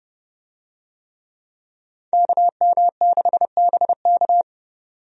Morse code, heard as a keyed tone, about 30 wpm, 700 Hz. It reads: KM6BK
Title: USA Call Signs - 5 Characters - 30wpm